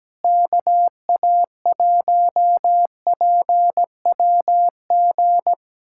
Morse code, heard as a keyed tone, about 17 wpm, 700 Hz. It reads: KA1PWG
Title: USA Callsign DE Callsign 17wpm